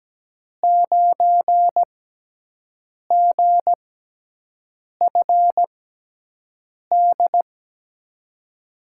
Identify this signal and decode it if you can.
Morse code, heard as a keyed tone, about 17 wpm, spaced as for 6 wpm, 700 Hz. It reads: 9GFD
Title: USA Call Signs - 5 Characters - 6x Character Spacing - 17wpm